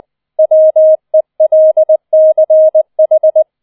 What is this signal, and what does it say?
Morse code, heard as a keyed tone, about 20 wpm, 630 Hz. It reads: WELCH
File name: 20wpm